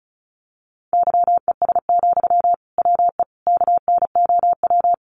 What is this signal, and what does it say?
Morse code, heard as a keyed tone, about 34 wpm, 700 Hz. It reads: YES, WE KNOW
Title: Sentences from Top 500 Words 35wpm